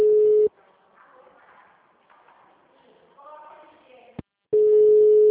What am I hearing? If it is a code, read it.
Ringback tone (Europe, 425 Hz), with speech.